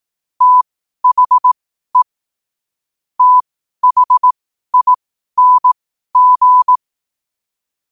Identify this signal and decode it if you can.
Morse code, heard as a keyed tone, about 18 wpm, spaced as for 12 wpm, 1000 Hz. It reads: THE THING
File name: qotdfiles / 12wpm